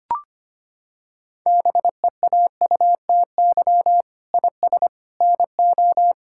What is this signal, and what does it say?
Morse code, heard as a keyed tone, about 25 wpm, 700 Hz. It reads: BEAUTY IS NO
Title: Sentences from Top 1600 Words 25wpm